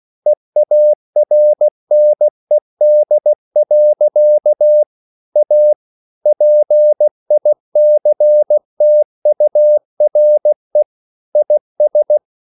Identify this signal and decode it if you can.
Morse code, heard as a keyed tone, about 16 wpm, 600 Hz. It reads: EARNED. A PICTURE IS
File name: CW / Practice / EnglishSayings16